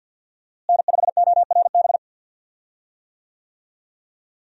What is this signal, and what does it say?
Morse code, heard as a keyed tone, about 49 wpm, 700 Hz. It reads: N5ORB
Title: USA Call Signs - 5 Characters - Repeated 2x - 50wpm